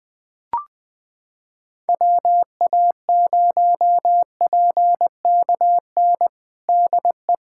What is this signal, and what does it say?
Morse code, heard as a keyed tone, about 21 wpm, 700 Hz. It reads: WA0PKN DE
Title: USA Callsign DE Callsign - 6 Characters - 20wpm